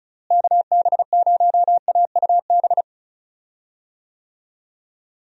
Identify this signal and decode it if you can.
Morse code, heard as a keyed tone, about 34 wpm, 700 Hz. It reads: KB0AUB
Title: USA Call Signs - 6 Characters - Repeated 3x - 35wpm